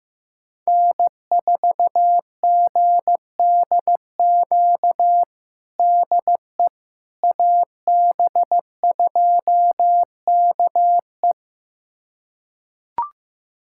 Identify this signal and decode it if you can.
Morse code, heard as a keyed tone, about 16 wpm, 700 Hz. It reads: N4GDQ DE AB2KE
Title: USA Callsign DE Callsign - 5 Characters - 15wpm